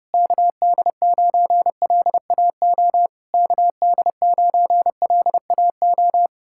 Morse code, heard as a keyed tone, about 34 wpm, 700 Hz. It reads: KD9LAO KD9LAO
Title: Speed-Racing Common QSO Phrases - 30wpm